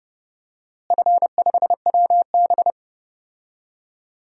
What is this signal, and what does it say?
Morse code, heard as a keyed tone, about 30 wpm, 700 Hz. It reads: F5WB
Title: DXCC-100 Call Signs - 4 Characters - Repeated 3x - 30wpm